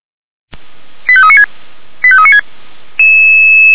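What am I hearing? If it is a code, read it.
Five-tone selective call (ZVEI), heard as quick runs of five tones, tones of about 70 ms. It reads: 85396 85396
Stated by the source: Five-tone ZVEI selective call received by SDR.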